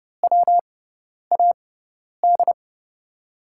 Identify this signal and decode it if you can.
Morse code, heard as a keyed tone, about 31 wpm, spaced as for 10 wpm, 700 Hz. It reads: WAD
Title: USA Call Signs - 6 Characters - 6x Character Spacing - 30wpm